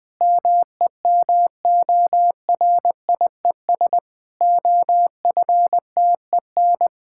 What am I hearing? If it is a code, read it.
Morse code, heard as a keyed tone, about 20 wpm, 700 Hz. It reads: MEMORIES OFTEN
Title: Difficult Sentences with Hints - 20wpm